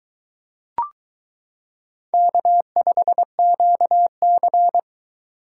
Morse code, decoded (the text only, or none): K5QC